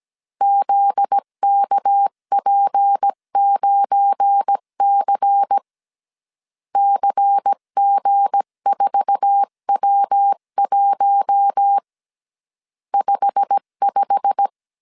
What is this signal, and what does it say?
Morse code, heard as a keyed tone, about 17 wpm, 800 Hz. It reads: ZKP9C CG4W1 55
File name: 1500C S B 16wpm